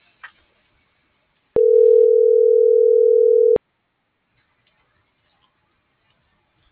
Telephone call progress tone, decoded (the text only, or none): ringback tone